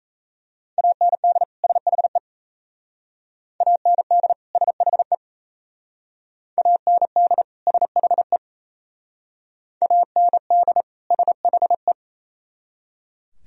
Morse code, heard as a keyed tone, about 35 wpm, 700 Hz. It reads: AND SHE AND SHE AND SHE AND SHE